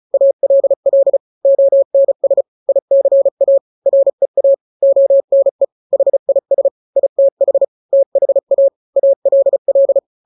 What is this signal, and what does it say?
Morse code, heard as a keyed tone, about 34 wpm, 550 Hz. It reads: ALL ONS ICA REA ONE HIS ITH THA ALL
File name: 3-grams 35 30 WPM